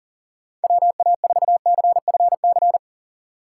Morse code, heard as a keyed tone, about 40 wpm, 700 Hz. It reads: WA4CFC
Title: USA Call Signs - 6 Characters - 4x Character Spacing - 40wpm